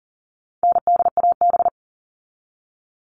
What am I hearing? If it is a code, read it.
Morse code, heard as a keyed tone, about 40 wpm, 700 Hz. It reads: NDAB
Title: All Cut Numbers - 4-Digit 40wpm